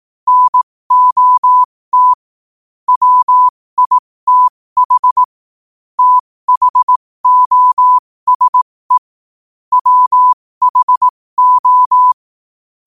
Morse code, heard as a keyed tone, about 18 wpm, spaced as for 15 wpm, 1000 Hz. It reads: NOT WITH THOSE WHO
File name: qotdfiles / 15wpm